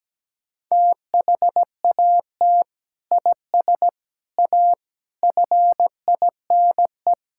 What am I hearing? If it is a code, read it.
Morse code, heard as a keyed tone, about 17 wpm, 700 Hz. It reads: THAT IS A FINE
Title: Sentences from Top 1000 Words - Review Part 5 - 17wpm